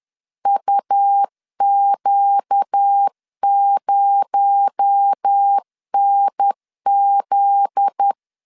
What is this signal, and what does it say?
Morse code, heard as a keyed tone, about 10 wpm, 800 Hz. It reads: UQ0NZ
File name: LEZIONI / 8 YJ45 / 8 YJ4 10wpm 2633